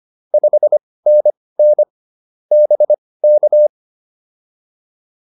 Morse code, heard as a keyed tone, about 25 wpm, spaced as for 18 wpm, 600 Hz. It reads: 5NN BK